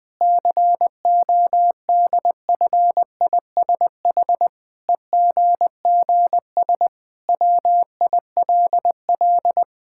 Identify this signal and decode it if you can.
Morse code, heard as a keyed tone, about 20 wpm, 700 Hz. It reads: CODFISH EGGS WILL
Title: Pangram Challenge v2 - 20wpm